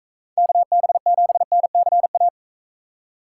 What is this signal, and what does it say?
Morse code, heard as a keyed tone, about 42 wpm, 700 Hz. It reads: KB7NCA